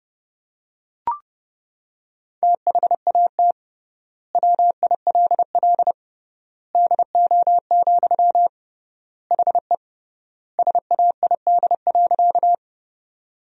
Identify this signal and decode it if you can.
Morse code, heard as a keyed tone, about 30 wpm, 700 Hz. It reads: THAT WILL DO, HE SAID.
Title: 3x Word-Spacing - Sentences from Top 100 Words - 30wpm